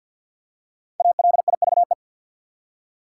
Morse code, heard as a keyed tone, about 53 wpm, 700 Hz. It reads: ALIVE
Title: Five Letter Words - 50wpm